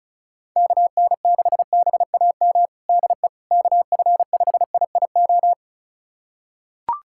Morse code, heard as a keyed tone, about 34 wpm, 700 Hz. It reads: KN6BAM DE KF5IIO
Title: USA Callsign DE Callsign - 6 Characters - 35wpm